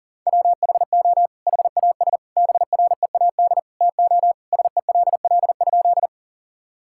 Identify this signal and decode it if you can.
Morse code, heard as a keyed tone, about 40 wpm, 700 Hz. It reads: WHO HAS BREAD TO SELL?
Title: Sentences from Top 1600 Words 40wpm